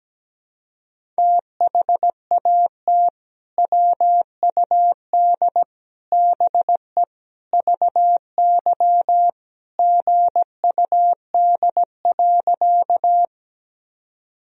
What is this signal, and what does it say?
Morse code, heard as a keyed tone, about 17 wpm, 700 Hz. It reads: THAT WUD BE VY GUD.
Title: Sentences from Top 200 Words and Abbreviations 17wpm